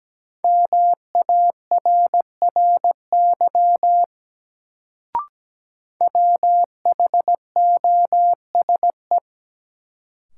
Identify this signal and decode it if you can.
Morse code, heard as a keyed tone, about 17 wpm, 700 Hz. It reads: MARRY WHOSE
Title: Sentences from Top 1200 Words - New Word Drill - 17wpm